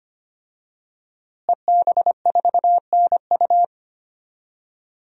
Morse code, heard as a keyed tone, about 25 wpm, 700 Hz. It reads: EB4NU
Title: DXCC-100 Call Signs - 5 Characters - Repeated 2x - 25wpm